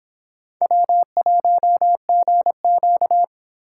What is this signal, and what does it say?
Morse code, heard as a keyed tone, about 26 wpm, 700 Hz. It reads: W1GQ